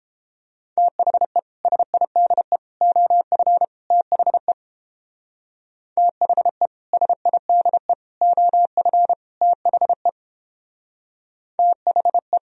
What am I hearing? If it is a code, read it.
Morse code, heard as a keyed tone, about 31 wpm, 700 Hz. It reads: THE SIDE OF THE THE SIDE OF THE THE